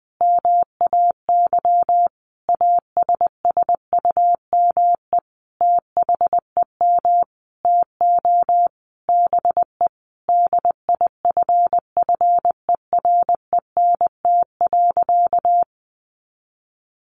Morse code, heard as a keyed tone, about 19 wpm, 700 Hz. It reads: MAY ASSUME THEM TO BE DIFFERENT.
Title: Sentences from Top 1500 Words 20wpm